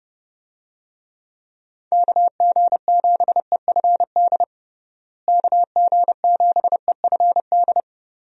Morse code, heard as a keyed tone, about 32 wpm, 700 Hz. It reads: KG7EFD KG7EFD
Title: USA Call Signs - 6 Characters - Repeated 3x - 30wpm